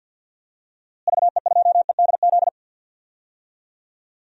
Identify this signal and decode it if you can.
Morse code, heard as a keyed tone, about 46 wpm, 700 Hz. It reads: VE2EDZ
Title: DXCC-100 Call Signs - 6 Characters - Repeated 3x - 50wpm